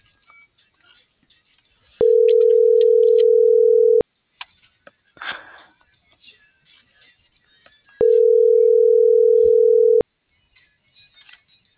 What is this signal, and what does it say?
Ringback tone (North America), 440 + 480 Hz.